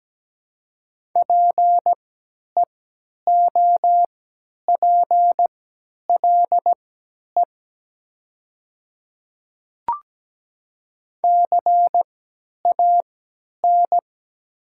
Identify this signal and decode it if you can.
Morse code, heard as a keyed tone, about 17 wpm, spaced as for 10 wpm, 700 Hz. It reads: PEOPLE CAN
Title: 3x Character Spacing - Top 100 Words - 17wpm